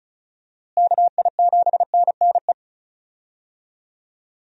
Morse code, heard as a keyed tone, about 36 wpm, 700 Hz. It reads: KI7NNE
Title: USA Call Signs - 6 Characters - Repeated 2x - 35wpm